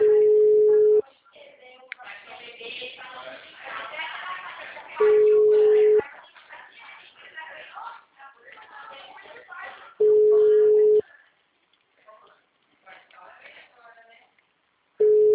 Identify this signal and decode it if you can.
Ringback tone (Europe, 425 Hz), with speech.